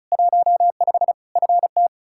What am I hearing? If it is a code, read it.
Morse code, heard as a keyed tone, about 36 wpm, 700 Hz. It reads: 15 FT
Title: Practice QSO - Exchange 2 - Info - 35wpm